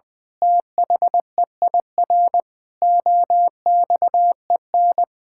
Morse code, heard as a keyed tone, about 21 wpm, 700 Hz. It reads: THEIR OXEN
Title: Pangram Challenge v2 - 20wpm